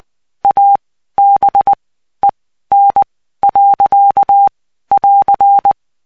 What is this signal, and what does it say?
Morse code, heard as a keyed tone, about 20 wpm, spaced as for 13 wpm, 800 Hz. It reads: ABEN. <AR>
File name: cw 75BPM